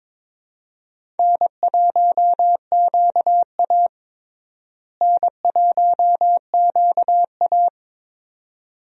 Morse code, heard as a keyed tone, about 22 wpm, 700 Hz. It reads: N1QA N1QA
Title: USA Call Signs - 4 Characters - Repeated 2x - 22wpm